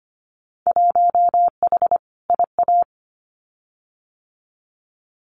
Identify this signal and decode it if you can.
Morse code, heard as a keyed tone, about 25 wpm, 700 Hz. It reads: 1H IA
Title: Winter Field Day Exchanges 25wpm